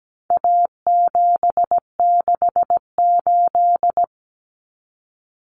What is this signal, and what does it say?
Morse code, heard as a keyed tone, about 17 wpm, 700 Hz. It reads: A768
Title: Common Cut Numbers - 4-Digit 17wpm